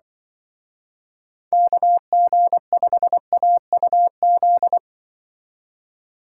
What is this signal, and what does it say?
Morse code, heard as a keyed tone, about 25 wpm, 700 Hz. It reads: KG5AUZ